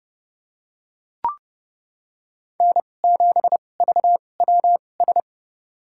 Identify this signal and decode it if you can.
Morse code, heard as a keyed tone, about 30 wpm, spaced as for 22 wpm, 700 Hz. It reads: N7VWS
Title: USA Call Signs - 5 Characters - 2x Character Spacing - 30wpm